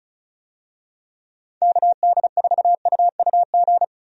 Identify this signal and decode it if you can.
Morse code, heard as a keyed tone, about 36 wpm, 700 Hz. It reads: KD4UUG
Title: USA Call Signs - 6 Characters - Repeated 2x - 35wpm